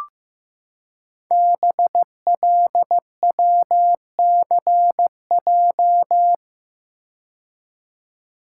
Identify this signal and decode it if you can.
Morse code, heard as a keyed tone, about 15 wpm, 700 Hz. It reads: BLWCJ
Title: Code Groups - 5 Letters - 15wpm